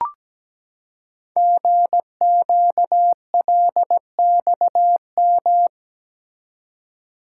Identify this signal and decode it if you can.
Morse code, heard as a keyed tone, about 17 wpm, 700 Hz. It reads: GQLXM
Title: Code Groups - 5 Letters - 17wpm